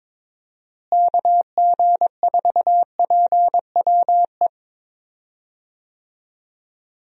Morse code, heard as a keyed tone, about 22 wpm, 700 Hz. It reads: KG4PWE
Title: USA Call Signs - 6 Characters - Repeated 3x - 22wpm